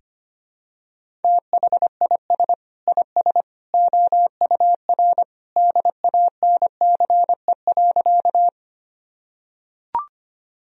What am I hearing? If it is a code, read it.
Morse code, heard as a keyed tone, about 25 wpm, 700 Hz. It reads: THIS IS OUR DANCE.
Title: Sentences from Top 1000 Words - Review Part 4 - 25wpm